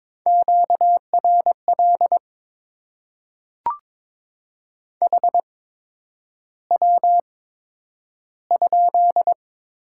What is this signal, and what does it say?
Morse code, heard as a keyed tone, about 22 wpm, spaced as for 9 wpm, 700 Hz. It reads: QRL HW?
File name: Top 20 QSO Elements - 8x Character Spacing - 22wpm